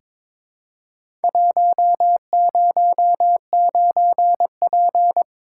Morse code, heard as a keyed tone, about 22 wpm, 700 Hz. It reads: 109P
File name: practice / 22wpm / Four Letter-Number Combinations - 22wpm